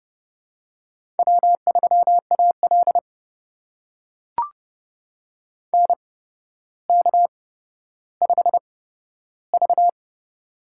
Morse code, heard as a keyed tone, about 30 wpm, spaced as for 13 wpm, 700 Hz. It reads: W3AL NK5V
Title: USA Call Signs - 4 Characters - 8x Character Spacing - 30wpm